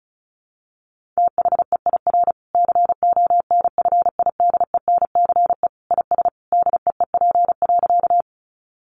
Morse code, heard as a keyed tone, about 34 wpm, 700 Hz. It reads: THEIR CONFIDENCE IS DEEP.
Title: Sentences from Top 1500 Words 35wpm